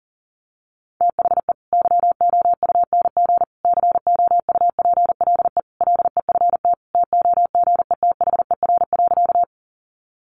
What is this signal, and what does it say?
Morse code, heard as a keyed tone, about 40 wpm, 700 Hz. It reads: THE YOUNG COUPLE LEFT TOGETHER.
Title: Sentences from Top 1100 Words 40wpm